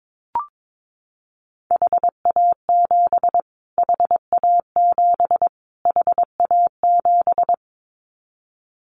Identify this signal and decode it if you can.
Morse code, heard as a keyed tone, about 22 wpm, 700 Hz. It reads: HA7 HA7 HA7
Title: Callsign Prefixes for Top 25 DXCC - Repeated 3x - 22wpm